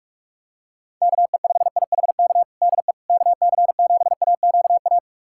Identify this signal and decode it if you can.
Morse code, heard as a keyed tone, about 46 wpm, 690 Hz. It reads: KE5IHK DE KC7AQA
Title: USA Callsign DE Callsign - 6 Characters - 45wpm